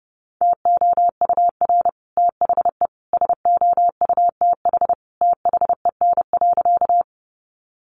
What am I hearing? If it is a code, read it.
Morse code, heard as a keyed tone, about 30 wpm, 700 Hz. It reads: TOUR THE SOUTH THEN.
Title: Sentences from Top 1400 Words 30wpm